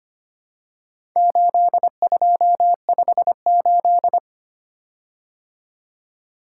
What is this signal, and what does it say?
Morse code, heard as a keyed tone, about 25 wpm, 700 Hz. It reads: 8258
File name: practice / 25wpm / Four Letter-Number Combinations - 25wpm